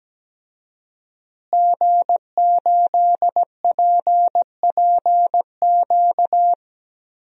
Morse code, heard as a keyed tone, about 17 wpm, 700 Hz. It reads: G8PPQ
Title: DXCC-100 Call Signs - 5 Characters - Repeated 3x - 17wpm